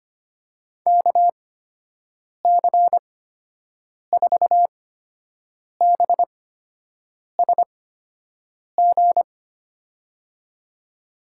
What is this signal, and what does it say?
Morse code, heard as a keyed tone, about 25 wpm, spaced as for 7 wpm, 700 Hz. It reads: KC4BSG
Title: USA Call Signs - 6 Characters - 8x Character Spacing - 25wpm